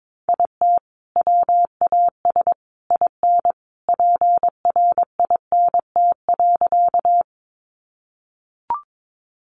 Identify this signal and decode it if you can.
Morse code, heard as a keyed tone, about 22 wpm, 700 Hz. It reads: IT WAS IN PRINT.
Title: Sentences from Top 1400 Words 22wpm